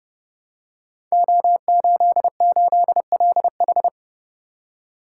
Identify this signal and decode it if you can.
Morse code, heard as a keyed tone, about 30 wpm, 700 Hz. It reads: O88LH
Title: Code Groups - 5 Letter:Numbers - 30wpm